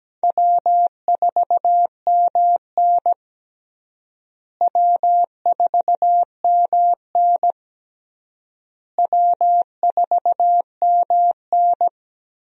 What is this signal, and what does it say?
Morse code, heard as a keyed tone, about 17 wpm, 700 Hz. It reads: W4MN W4MN W4MN
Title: USA Call Signs - 4 Characters - Repeated 3x - 17wpm